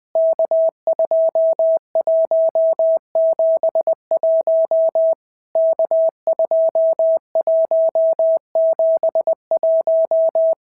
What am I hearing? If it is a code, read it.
Morse code, heard as a keyed tone, about 20 wpm, 650 Hz. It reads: K2171 K2171
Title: POTA QSO Practice - 20wpm